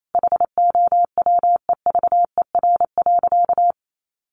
Morse code, heard as a keyed tone, about 28 wpm, 700 Hz. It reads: HOWEVER.
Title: Sentences from Top 1500 Words 28wpm